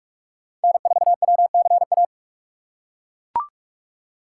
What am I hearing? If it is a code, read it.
Morse code, heard as a keyed tone, about 42 wpm, 700 Hz. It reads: N 4 W C A
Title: USA Call Signs - 5 Characters - 6x Character Spacing - 45wpm